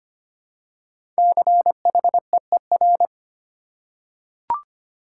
Morse code, heard as a keyed tone, about 25 wpm, 700 Hz. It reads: CHEER
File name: 5 Letter Words - First 3 Letters Given - 25wpm